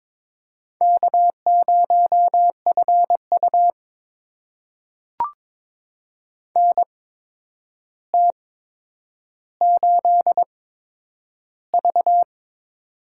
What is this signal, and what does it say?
Morse code, heard as a keyed tone, about 22 wpm, spaced as for 9 wpm, 700 Hz. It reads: K0FU NT8V
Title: USA Call Signs - 4 Characters - 8x Character Spacing - 22wpm